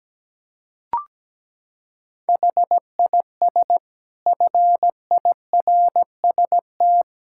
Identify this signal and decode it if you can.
Morse code, heard as a keyed tone, about 17 wpm, 700 Hz. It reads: HIS FIRST
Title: Sentences from Top 1600 Words 17wpm